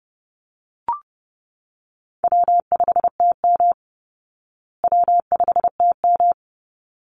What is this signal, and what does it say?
Morse code, heard as a keyed tone, about 30 wpm, 700 Hz. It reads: W5TM W5TM
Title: Easy Call Signs Repeated 2x - 30wpm